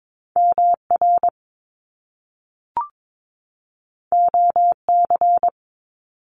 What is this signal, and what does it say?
Morse code, heard as a keyed tone, about 22 wpm, 700 Hz. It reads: MR OC
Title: Top Two Letter Combinations in Words 22wpm